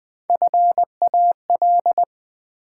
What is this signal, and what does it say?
Morse code, heard as a keyed tone, about 20 wpm, 700 Hz. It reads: FAL
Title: Build A Word - 1 Letter At A Time - 4-Letter Words - 20wpm